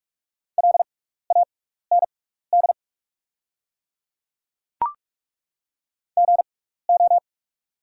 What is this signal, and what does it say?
Morse code, heard as a keyed tone, about 43 wpm, spaced as for 15 wpm, 690 Hz. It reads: LAND GO
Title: 6x Character Spacing - Top 300 Words - 45wpm